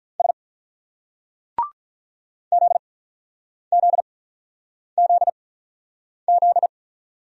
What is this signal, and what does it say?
Morse code, heard as a keyed tone, about 44 wpm, spaced as for 9 wpm, 700 Hz. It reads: S ZZZZ